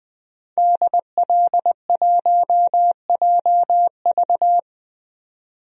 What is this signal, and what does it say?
Morse code, heard as a keyed tone, about 21 wpm, 700 Hz. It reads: DL1JV
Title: DXCC-100 Call Signs - 5 Characters - 20wpm